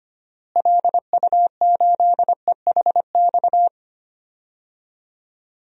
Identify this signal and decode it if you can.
Morse code, heard as a keyed tone, about 25 wpm, 700 Hz. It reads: LU8EHX
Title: DXCC-100 Call Signs - 6 Characters - Repeated 3x - 25wpm